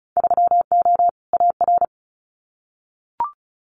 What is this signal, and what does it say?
Morse code, heard as a keyed tone, about 36 wpm, 700 Hz. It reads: 3O AR
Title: Winter Field Day Exchanges Encore 35wpm